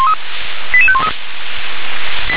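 Five-tone selective call (ZVEI), heard as quick runs of five tones, tones of about 70 ms. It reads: 88513
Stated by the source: ZVEI five-tone selective call 88716 received with an SDR.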